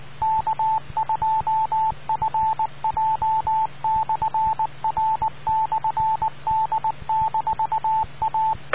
Morse code, heard as a keyed tone, about 19 wpm, 880 Hz. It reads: K2FJ/R/D-A